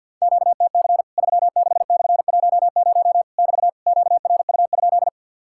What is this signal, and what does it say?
Morse code, heard as a keyed tone, about 50 wpm, 700 Hz. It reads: QTC 36/10 <BT> QRU?
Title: Worked All Europe DX Contest - QTC Exchange with Repeats - 50wpm